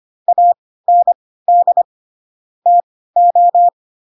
Morse code, heard as a keyed tone, about 25 wpm, spaced as for 16 wpm, 700 Hz. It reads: AND TO
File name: Pinocchio / Pinocchio ch3 10 25 ik1hge